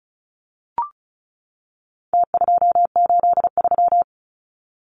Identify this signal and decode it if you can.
Morse code, heard as a keyed tone, about 34 wpm, 700 Hz. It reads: T283
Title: Numbers - Contest Style - 4-Digit 35wpm